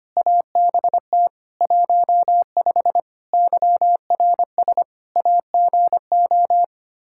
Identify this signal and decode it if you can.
Morse code, heard as a keyed tone, about 25 wpm, 700 Hz. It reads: ABT 15 YRS AGO
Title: Practice QSO - Exchange 3 - More Info - 25wpm